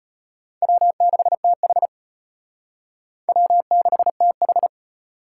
Morse code, heard as a keyed tone, about 32 wpm, 700 Hz. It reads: W6TH W6TH